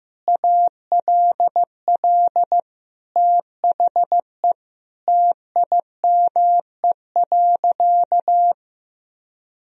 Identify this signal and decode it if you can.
Morse code, heard as a keyed tone, about 16 wpm, 700 Hz. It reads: ALL THE TIME.
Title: Sentences from Top 1000 Words - Review Part 3 - 15wpm